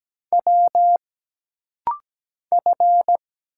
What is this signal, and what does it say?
Morse code, heard as a keyed tone, about 17 wpm, 700 Hz. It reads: W F
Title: Single Letters - Rapid-Fire 17wpm